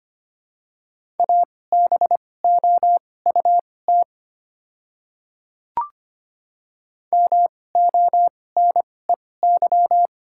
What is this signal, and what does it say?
Morse code, heard as a keyed tone, about 25 wpm, spaced as for 18 wpm, 700 Hz. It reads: ABOUT MONEY
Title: 2x Character Spacing - Top 500 Words - 25wpm